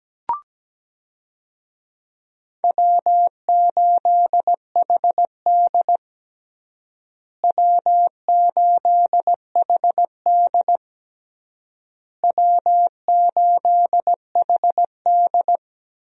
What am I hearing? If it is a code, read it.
Morse code, heard as a keyed tone, about 17 wpm, 700 Hz. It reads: W8HD W8HD W8HD
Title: USA Call Signs - 4 Characters - Repeated 3x - 17wpm